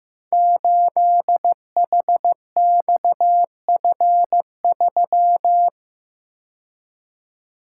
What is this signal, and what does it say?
Morse code, heard as a keyed tone, about 15 wpm, 700 Hz. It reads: 8HXF3
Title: Code Groups - 5 Letter:Numbers - 15wpm